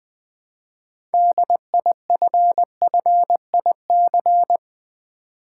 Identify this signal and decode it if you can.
Morse code, heard as a keyed tone, about 20 wpm, 700 Hz. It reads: DIFFIC
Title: Build A Word - 3 Letters At A Time - 9-Letter Words - 20wpm